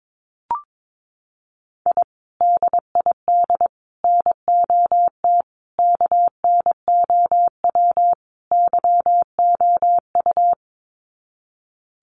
Morse code, heard as a keyed tone, about 22 wpm, 700 Hz. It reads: I DID NOT KNOW YOU
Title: Sets of 5 Words Encore2 22wpm